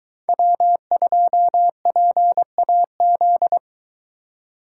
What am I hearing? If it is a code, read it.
Morse code, heard as a keyed tone, about 22 wpm, 700 Hz. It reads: W2PAZ